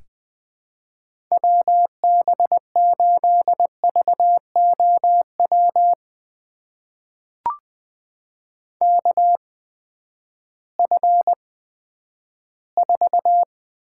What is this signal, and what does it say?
Morse code, heard as a keyed tone, about 20 wpm, spaced as for 11 wpm, 700 Hz. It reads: WB8VOW KF4
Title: USA Call Signs - 6 Characters - 8x Character Spacing - 20wpm